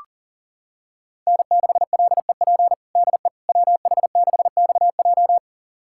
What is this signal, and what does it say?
Morse code, heard as a keyed tone, about 40 wpm, 700 Hz. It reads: N6LEP DE WH6XJ
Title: USA Callsign DE Callsign - 5 Characters - 40wpm